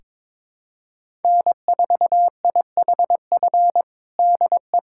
Morse code, heard as a keyed tone, about 22 wpm, 700 Hz. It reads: N4IHF DE
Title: USA Callsign DE Callsign - 5 Characters - 22wpm